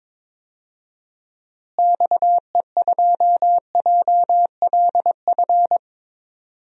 Morse code, heard as a keyed tone, about 22 wpm, 700 Hz. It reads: XE2JLF
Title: DXCC-100 Call Signs - 6 Characters - Repeated 3x - 22wpm